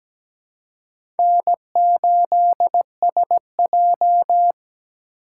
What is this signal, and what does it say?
Morse code, heard as a keyed tone, about 17 wpm, 700 Hz. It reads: N8SJ
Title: USA Call Signs - 4 Characters - 3x Character Spacing - 17wpm